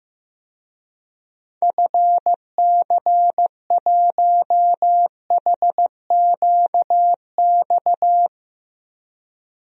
Morse code, heard as a keyed tone, about 16 wpm, 700 Hz. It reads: FC1HQX
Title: DXCC-100 Call Signs - 6 Characters - 15wpm